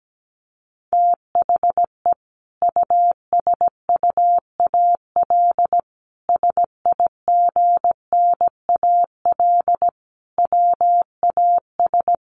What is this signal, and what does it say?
Morse code, heard as a keyed tone, about 17 wpm, 700 Hz. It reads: THE USUAL SIGNAL WAS
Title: Sentences from Top 1400 Words 17wpm